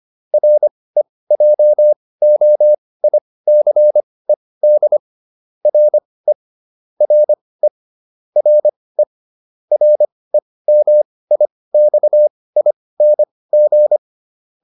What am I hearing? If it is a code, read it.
Morse code, heard as a keyed tone, about 25 wpm, spaced as for 18 wpm, 600 Hz. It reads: REJOICED RE RE RE REMIXING